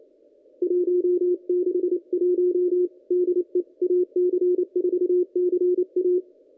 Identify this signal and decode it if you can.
Morse code, heard as a keyed tone, about 28 wpm, 360 Hz. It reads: 161 DE AC4CA
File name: contests / ss-cw-k5tr-op-n5rz -1800